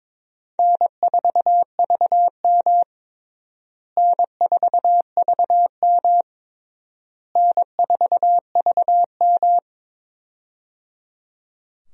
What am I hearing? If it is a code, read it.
Morse code, heard as a keyed tone, about 22 wpm, 700 Hz. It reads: N4VM N4VM N4VM
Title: USA Call Signs - 4 Characters - Repeated 3x - 22wpm